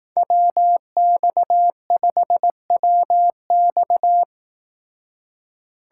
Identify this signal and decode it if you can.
Morse code, heard as a keyed tone, about 18 wpm, 700 Hz. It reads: WX5WX
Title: Speed Racing - USA Call Signs 15wpm